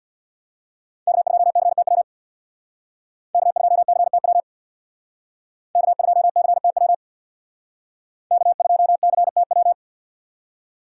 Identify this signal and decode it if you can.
Morse code, heard as a keyed tone, about 62 wpm, 700 Hz. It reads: K2CTW K2CTW K2CTW K2CTW
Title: Speed Racing - USA 1x3 Call Signs 50wpm